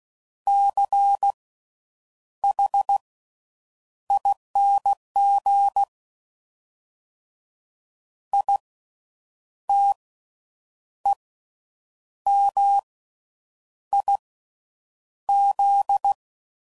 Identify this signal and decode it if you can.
Morse code, heard as a keyed tone, about 16 wpm, spaced as for 7 wpm, 800 Hz. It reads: CHING ITEMIZ
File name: Slow With Faster Ings